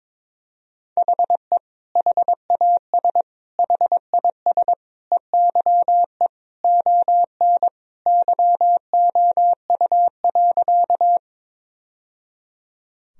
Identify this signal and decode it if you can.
Morse code, heard as a keyed tone, about 22 wpm, 700 Hz. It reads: HE HAS HIS EYE ON YOU.
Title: Sentences from Top 1000 Words - Review Part 1 - 22wpm